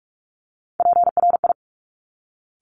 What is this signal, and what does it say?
Morse code, heard as a keyed tone, about 46 wpm, 700 Hz. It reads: PRI
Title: Top Three Letter Combinations in Words 45wpm